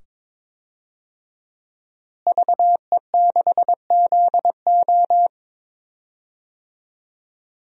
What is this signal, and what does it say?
Morse code, heard as a keyed tone, about 21 wpm, 700 Hz. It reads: VE6ZO
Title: DXCC-100 Call Signs - 5 Characters - Repeated 2x - 22wpm